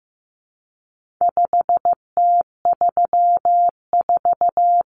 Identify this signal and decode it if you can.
Morse code, heard as a keyed tone, about 15 wpm, 700 Hz. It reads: 5T34
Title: Common Cut Numbers - 4-Digit 15wpm